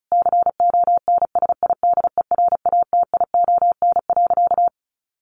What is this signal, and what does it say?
Morse code, heard as a keyed tone, about 34 wpm, 700 Hz. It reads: CONSIDERATION.
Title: Sentences from Top 1500 Words 35wpm